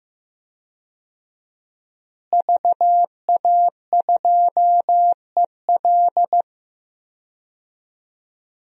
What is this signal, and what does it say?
Morse code, heard as a keyed tone, about 15 wpm, 700 Hz. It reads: VA2EL
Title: DXCC-100 Call Signs - 5 Characters - Repeated 2x - 15wpm